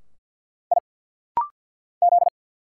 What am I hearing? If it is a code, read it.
Morse code, heard as a keyed tone, about 50 wpm, 700 Hz. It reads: I Z
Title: Single Letter-Number Alt - Rapid-Fire 50wpm